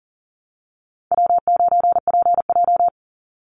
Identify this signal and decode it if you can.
Morse code, heard as a keyed tone, about 40 wpm, 700 Hz. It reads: W9PJ
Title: Easy Call Signs Repeated 2x - 40wpm